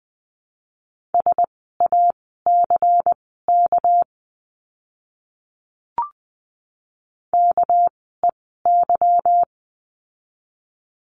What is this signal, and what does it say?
Morse code, heard as a keyed tone, about 20 wpm, spaced as for 15 wpm, 700 Hz. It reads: SACK KEY
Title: 199 Lesson 32f - Letter k - Words 20wpm